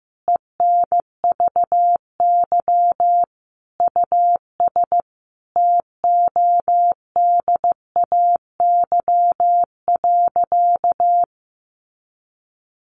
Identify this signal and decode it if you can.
Morse code, heard as a keyed tone, about 15 wpm, 700 Hz. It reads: ENVY US TODAY.
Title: A Sound Of Thunder 15wpm